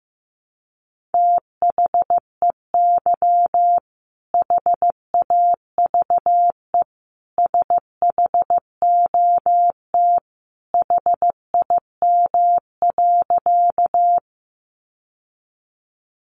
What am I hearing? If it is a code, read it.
Morse code, heard as a keyed tone, about 15 wpm, 700 Hz. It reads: THEY HAVE SHOT HIM.
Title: Sentences from Top 1100 Words 15wpm